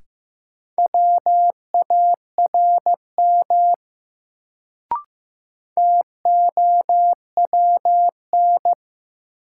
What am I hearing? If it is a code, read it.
Morse code, heard as a keyed tone, about 15 wpm, 700 Hz. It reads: WARM TOWN
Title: Sentences from Top 500 Words - New Word Drill - 15wpm